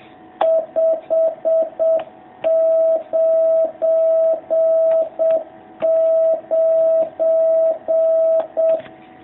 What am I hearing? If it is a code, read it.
Morse code, heard as a keyed tone, about 7 wpm, 660 Hz. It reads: 599
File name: jh8cbh / 2019cw